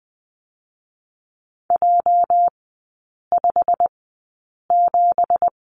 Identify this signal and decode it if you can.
Morse code, heard as a keyed tone, about 20 wpm, spaced as for 8 wpm, 700 Hz. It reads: J57
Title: 218 Lesson 35e - Number 0 - Three Characters 20wpm